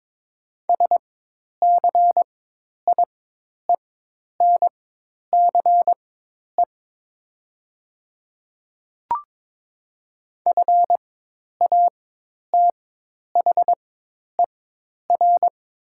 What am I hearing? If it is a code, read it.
Morse code, heard as a keyed tone, about 22 wpm, spaced as for 10 wpm, 700 Hz. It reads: SCIENCE FATHER
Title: 4x Character Spacing - Top 500 Words - 22wpm